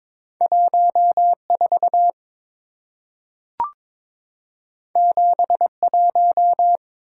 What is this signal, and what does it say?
Morse code, heard as a keyed tone, about 22 wpm, 690 Hz. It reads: 14 71
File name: practice / 22wpm / Two Letter-Number Combinations - 22wpm